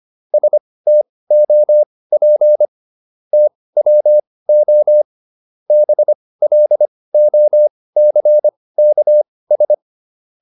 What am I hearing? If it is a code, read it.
Morse code, heard as a keyed tone, about 25 wpm, spaced as for 18 wpm, 600 Hz. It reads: STOP TWO BLOCKS